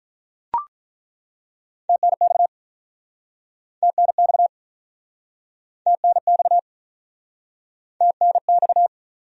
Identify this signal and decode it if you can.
Morse code, heard as a keyed tone, about 47 wpm, 700 Hz. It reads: TNX TNX TNX TNX